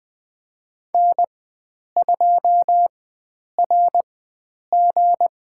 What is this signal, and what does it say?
Morse code, heard as a keyed tone, about 20 wpm, spaced as for 9 wpm, 700 Hz. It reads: N2RG
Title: USA Call Signs - 4x Character Spacing - 20wpm